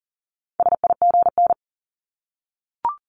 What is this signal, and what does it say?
Morse code, heard as a keyed tone, about 41 wpm, 700 Hz. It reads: SIGN
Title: Multi-letter Phonemes 40wpm